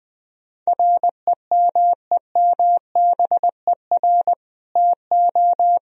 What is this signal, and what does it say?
Morse code, heard as a keyed tone, about 21 wpm, 700 Hz. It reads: REMEMBER TO
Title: Sentences from Top 1000 Words - Review Part 3 - 20wpm